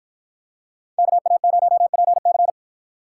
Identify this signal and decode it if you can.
Morse code, heard as a keyed tone, about 51 wpm, 700 Hz. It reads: KA0PC